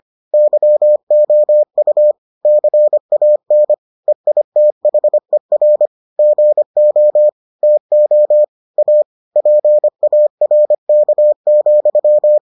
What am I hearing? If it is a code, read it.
Morse code, heard as a keyed tone, about 25 wpm, 600 Hz. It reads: YOU CAN EITHER GO TO A PARK,